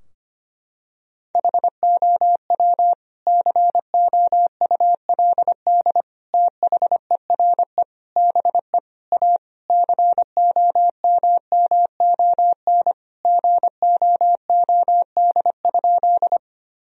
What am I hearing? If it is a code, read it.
Morse code, heard as a keyed tone, about 25 wpm, 700 Hz. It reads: HOW COULD THERE BE A COMMON GOOD?
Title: Sentences from Top 1000 Words - Review Part 3 - 25wpm